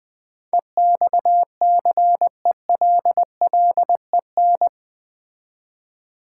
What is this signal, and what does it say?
Morse code, heard as a keyed tone, about 20 wpm, 700 Hz. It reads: EXCELLEN
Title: Build A Word - 4 Letters At A Time - 10-Letter Words - 20wpm